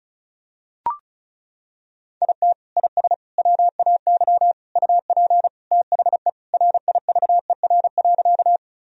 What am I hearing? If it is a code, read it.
Morse code, heard as a keyed tone, about 36 wpm, 700 Hz. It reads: IT IS WAY UP THE RIVER.
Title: Sentences from Top 1000 Words - Review Part 1 - 35wpm